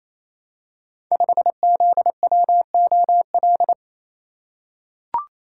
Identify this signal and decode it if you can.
Morse code, heard as a keyed tone, about 28 wpm, 700 Hz. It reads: 5ZWOL
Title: Code Groups - 5 Letter:Numbers - 28wpm